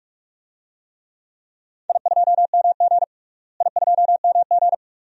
Morse code, heard as a keyed tone, about 41 wpm, 700 Hz. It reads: I2MG I2MG
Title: DXCC-100 Call Signs - 4 Characters - Repeated 2x - 45wpm